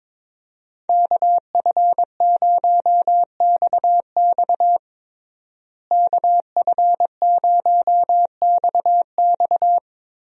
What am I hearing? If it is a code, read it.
Morse code, heard as a keyed tone, about 22 wpm, 700 Hz. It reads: KF0XX KF0XX
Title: USA Call Signs - 5 Characters - Repeated 2x - 22wpm